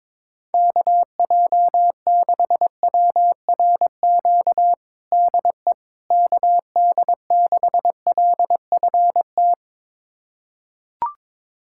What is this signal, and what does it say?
Morse code, heard as a keyed tone, about 22 wpm, 700 Hz. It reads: KJ6WRQ DE KD6LFT
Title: USA Callsign DE Callsign - 6 Characters - 22wpm